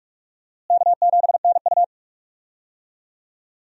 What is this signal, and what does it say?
Morse code, heard as a keyed tone, about 46 wpm, 700 Hz. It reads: K7NU
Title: USA Call Signs - 4 Characters - Repeated 3x - 45wpm